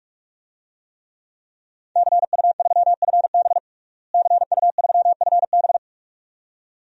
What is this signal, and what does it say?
Morse code, heard as a keyed tone, about 41 wpm, 700 Hz. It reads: CU3FB CU3FB
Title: DXCC-100 Call Signs - 5 Characters - Repeated 2x - 45wpm